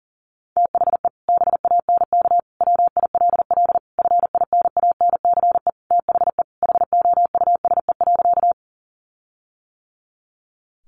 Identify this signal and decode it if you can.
Morse code, heard as a keyed tone, about 40 wpm, 700 Hz. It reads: THE BANK WILL FINANCE THE HOUSE.
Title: Sentences from Top 1500 Words 40wpm